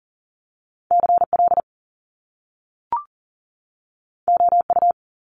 Morse code, heard as a keyed tone, about 40 wpm, 700 Hz. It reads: CL OU
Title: Top Two Letter Combinations in Words 40wpm